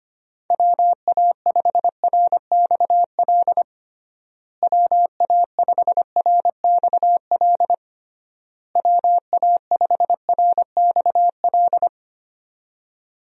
Morse code, heard as a keyed tone, about 25 wpm, 700 Hz. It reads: WA5RXL WA5RXL WA5RXL
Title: USA Call Signs - 6 Characters - Repeated 3x - 25wpm